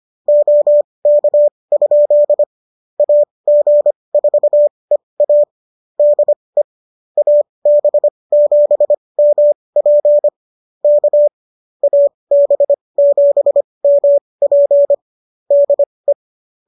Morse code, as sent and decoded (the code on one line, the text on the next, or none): --- -.- ..--.. / .- --. ....- . .- / -.. / .- -... --... -- .--. / -.- / .- -... --... -- .--. / -.. .
OK? AG4EA DE AB7MP K AB7MP DE